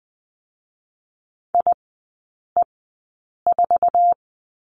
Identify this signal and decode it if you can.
Morse code, heard as a keyed tone, about 20 wpm, spaced as for 8 wpm, 700 Hz. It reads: IE4
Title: 053 Lesson 10e - Letter r - Three Characters 20wpm